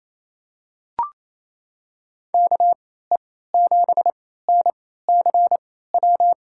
Morse code, heard as a keyed tone, about 28 wpm, spaced as for 16 wpm, 700 Hz. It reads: KE7NCW
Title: USA Call Signs - 6 Characters - 3x Character Spacing - 28wpm